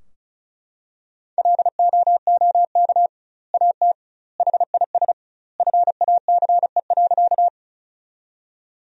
Morse code, heard as a keyed tone, about 36 wpm, 700 Hz. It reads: LOOK AT HIS FACE.
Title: 2x Word-Spacing - Sentences from Top 300 Words - 35wpm